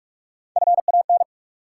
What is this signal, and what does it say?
Morse code, heard as a keyed tone, about 43 wpm, 700 Hz. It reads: FAN
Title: Sentences from Top 1000 Words - Review Part 4 - New Word Drill - 45wpm